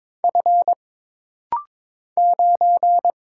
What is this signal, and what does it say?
Morse code, heard as a keyed tone, about 22 wpm, 700 Hz. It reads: F 9
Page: This rapid-fire timing is geared to push your ICR (Instant Character Recognition) skills.